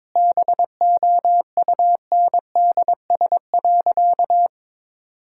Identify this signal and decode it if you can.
Morse code, heard as a keyed tone, about 22 wpm, 700 Hz. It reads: BOUNDS.
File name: Difficult Sentences with Hints - 22wpm